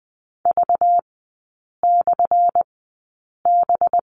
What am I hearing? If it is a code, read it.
Morse code, heard as a keyed tone, about 20 wpm, spaced as for 8 wpm, 700 Hz. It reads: V/B
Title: 191 Lesson 31e - Letter v - Three Characters 20wpm